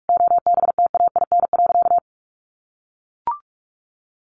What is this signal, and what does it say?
Morse code, heard as a keyed tone, about 44 wpm, 700 Hz. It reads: OBTAIN.
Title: Sentences from Top 1300 Words 45wpm